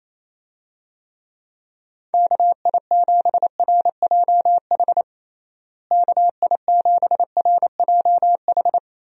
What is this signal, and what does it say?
Morse code, heard as a keyed tone, about 28 wpm, 700 Hz. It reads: KI7RJH KI7RJH
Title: USA Call Signs - 6 Characters - Repeated 3x - 28wpm